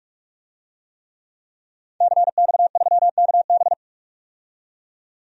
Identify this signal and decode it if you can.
Morse code, heard as a keyed tone, about 42 wpm, 700 Hz. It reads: CX3KB
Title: DXCC-100 Call Signs - 5 Characters - 45wpm